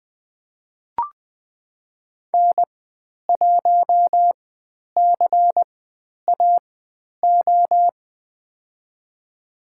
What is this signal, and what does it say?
Morse code, heard as a keyed tone, about 20 wpm, spaced as for 10 wpm, 700 Hz. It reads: N1CAO
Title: 095 Lesson 16g - Letter c - Call Signs 20wpm